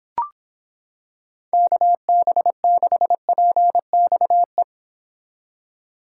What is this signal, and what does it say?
Morse code, heard as a keyed tone, about 26 wpm, 700 Hz. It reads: KB6PXE